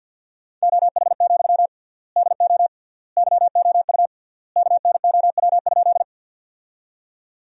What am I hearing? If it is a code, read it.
Morse code, heard as a keyed tone, about 50 wpm, 700 Hz. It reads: OH, DO YOU KNOW?
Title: 3x Word-Spacing - Sentences from Top 500 Words - 50wpm